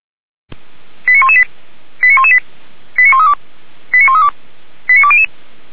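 Five-tone selective call (ZVEI), heard as quick runs of five tones, tones of about 70 ms. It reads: 89118 89118 89132 89132 89200
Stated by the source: ZVEI five-tone call received by SDR, archived by feuerwehr-orlamuende.org (Orlamünde fire brigade).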